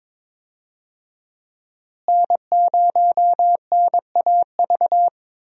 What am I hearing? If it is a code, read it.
Morse code, heard as a keyed tone, about 22 wpm, 700 Hz. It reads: N0NAV
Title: USA Call Signs - 5 Characters - Repeated 3x - 22wpm